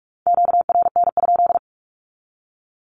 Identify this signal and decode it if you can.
Morse code, heard as a keyed tone, about 44 wpm, 700 Hz. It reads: QRN?
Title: Q-Codes 45wpm